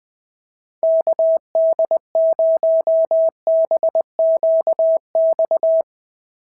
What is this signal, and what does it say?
Morse code, heard as a keyed tone, about 20 wpm, 690 Hz. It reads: KD0BQX
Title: SOTA QSO Practice 20wpm